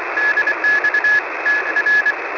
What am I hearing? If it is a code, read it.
Morse code, heard as a keyed tone, about 24 wpm, 1700 Hz, with radio noise. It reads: DX TR
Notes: Gravações brutas realizadas para o canal Enciclopédia do Rádio no YouTube.